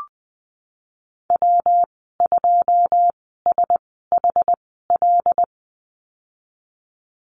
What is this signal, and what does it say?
Morse code, heard as a keyed tone, about 20 wpm, spaced as for 14 wpm, 700 Hz. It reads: W2SHL
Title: 186 Lesson 30g - Letter b - Call Signs 20wpm